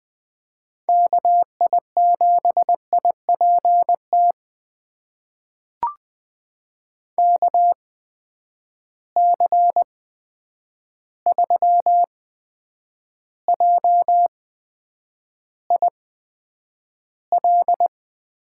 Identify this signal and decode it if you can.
Morse code, heard as a keyed tone, about 20 wpm, spaced as for 9 wpm, 700 Hz. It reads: KI7IPT KC3JIL
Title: USA Call Signs - 6 Characters - 8x Character Spacing - 20wpm